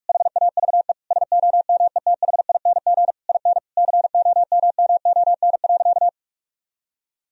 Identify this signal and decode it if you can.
Morse code, heard as a keyed tone, about 46 wpm, 700 Hz. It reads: HAVE SOMETHING IN COMMON.
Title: Sentences from Top 1000 Words - Review Part 3 - 45wpm